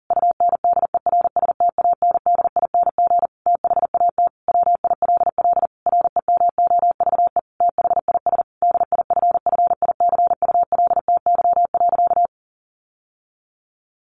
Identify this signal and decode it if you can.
Morse code, heard as a keyed tone, about 37 wpm, 700 Hz. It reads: UNDERSTANDING THAT WILL REMOVE THIS DIFFICULTY.
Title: Sentences from Top 1500 Words 40wpm